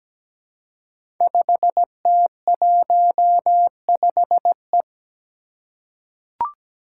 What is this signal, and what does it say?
Morse code, heard as a keyed tone, about 17 wpm, 700 Hz. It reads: 5T15E
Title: Five Challenging Letter-Number Combinations 17wpm